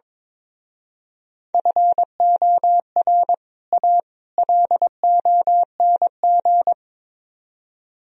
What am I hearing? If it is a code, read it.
Morse code, heard as a keyed tone, about 24 wpm, 700 Hz. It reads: FOR A LONG
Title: Speed-Racing - Sets of 3 Words 22wpm